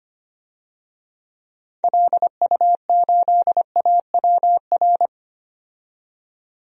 Morse code, heard as a keyed tone, about 25 wpm, 700 Hz. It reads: LU8AWR
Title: DXCC-100 Call Signs - 6 Characters - 25wpm